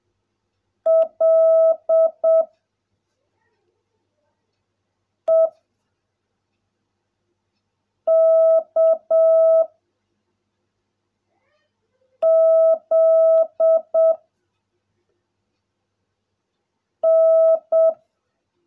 Morse code, heard as a keyed tone, about 7 wpm, 660 Hz. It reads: L E K Z N
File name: jh8cbh / 2021cw / cw16a